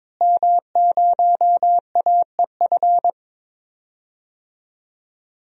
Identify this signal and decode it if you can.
Morse code, heard as a keyed tone, about 22 wpm, 700 Hz. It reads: M0AEF
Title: DXCC-100 Call Signs - 5 Characters - Repeated 2x - 22wpm